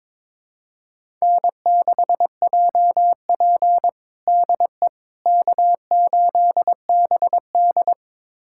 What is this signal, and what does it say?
Morse code, heard as a keyed tone, about 22 wpm, 700 Hz. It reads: N6JP DE K8BD
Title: USA Callsign DE Callsign - 4 Characters - 22wpm